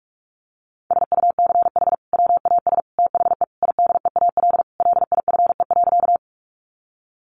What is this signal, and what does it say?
Morse code, heard as a keyed tone, about 44 wpm, 700 Hz. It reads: SUCH WAS THE IDEAL LIFE.
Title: Sentences from Top 1400 Words 45wpm